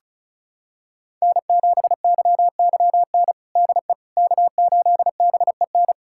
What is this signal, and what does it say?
Morse code, heard as a keyed tone, about 36 wpm, 690 Hz. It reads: N7YYN DE K8BEN
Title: USA Callsign DE Callsign - 5 Characters - 35wpm